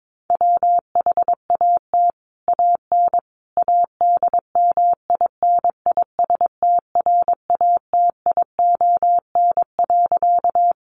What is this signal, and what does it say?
Morse code, heard as a keyed tone, about 22 wpm, 700 Hz. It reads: WHAT AN ADMINISTRATION.
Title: Sentences from Top 1300 Words 22wpm